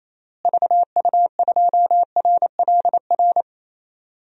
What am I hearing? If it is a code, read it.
Morse code, heard as a keyed tone, about 28 wpm, 700 Hz. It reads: VU2RLR
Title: DXCC-100 Call Signs - 6 Characters - Repeated 3x - 28wpm